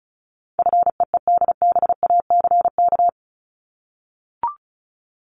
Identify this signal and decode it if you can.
Morse code, heard as a keyed tone, about 34 wpm, 700 Hz. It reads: FEEDBACK
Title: Top 1501-1750 Words 35wpm